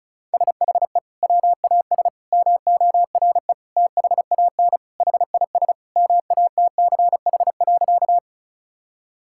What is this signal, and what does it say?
Morse code, heard as a keyed tone, about 34 wpm, 700 Hz. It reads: SHE WAS MORE THAN HIS MATCH.
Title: Sentences from Top 1600 Words 35wpm